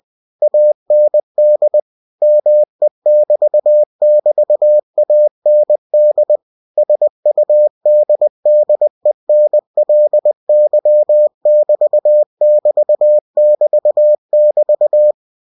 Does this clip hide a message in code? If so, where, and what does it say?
Yes: Morse code from 0.4 to 15.1 s, heard as a keyed tone, about 20 wpm, 600 Hz. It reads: AND ME <BT> <BT> AND SUDDENLY <BT> <BT> <BT> <BT>